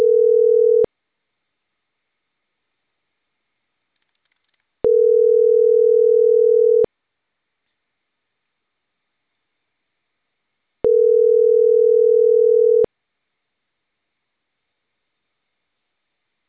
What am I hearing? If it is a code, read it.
Ringback tone (North America), 440 + 480 Hz.